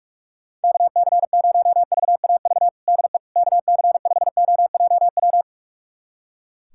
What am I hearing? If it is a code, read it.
Morse code, heard as a keyed tone, about 45 wpm, 700 Hz. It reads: KC0VAV DE KC5OJW
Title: USA Callsign DE Callsign 45wpm